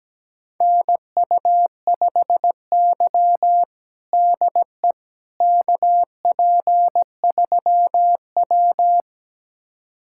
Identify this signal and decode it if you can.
Morse code, heard as a keyed tone, about 17 wpm, 700 Hz. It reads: NU5Y DE KP3W
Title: USA Callsign DE Callsign - 4 Characters - 17wpm